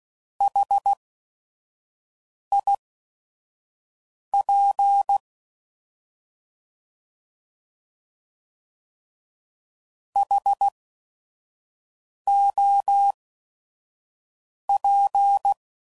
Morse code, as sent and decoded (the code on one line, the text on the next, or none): .... .. .--. / .... --- .--.
HIP HOP